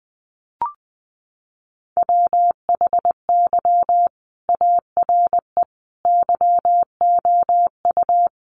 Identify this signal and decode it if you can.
Morse code, heard as a keyed tone, about 20 wpm, 700 Hz. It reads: WHY ARE YOU
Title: Sentences from Top 800 Words 20wpm